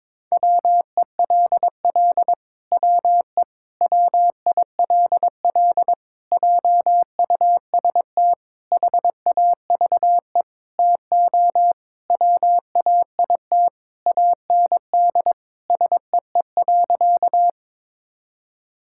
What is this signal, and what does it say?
Morse code, heard as a keyed tone, about 22 wpm, 700 Hz. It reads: WELL WE WILL JUST HAVE TO WAIT AND SEE.
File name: Sentences with 7-Word Phrases - 22wpm